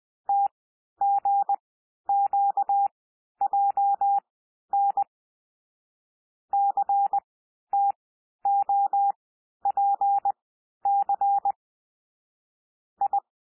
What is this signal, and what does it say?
Morse code, heard as a keyed tone, about 20 wpm, spaced as for 11 wpm, 800 Hz. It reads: TGQJN CTOPC I
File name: cwmp3 / L3 SOUND LETTERS 20 wpm